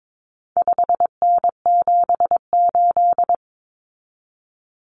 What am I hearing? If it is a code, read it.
Morse code, heard as a keyed tone, about 22 wpm, 700 Hz. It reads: 5N78
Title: Common Cut Numbers - 4-Digit 22wpm